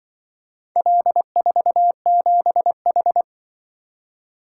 Morse code, heard as a keyed tone, about 28 wpm, 700 Hz. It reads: L47H